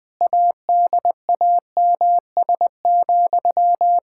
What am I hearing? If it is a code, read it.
Morse code, heard as a keyed tone, about 20 wpm, 700 Hz. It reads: ADAMS,
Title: Practice QSO - Exchange 1 - Beginning - 20wpm